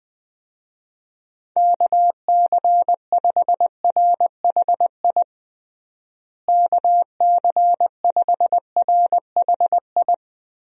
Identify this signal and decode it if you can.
Morse code, heard as a keyed tone, about 20 wpm, 700 Hz. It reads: KC5RHI KC5RHI
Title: USA Call Signs - 6 Characters - Repeated 3x - 20wpm